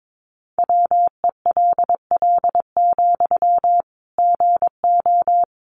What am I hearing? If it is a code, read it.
Morse code, heard as a keyed tone, about 22 wpm, 700 Hz. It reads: WELL, GO
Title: Sentences from Top 1100 Words 22wpm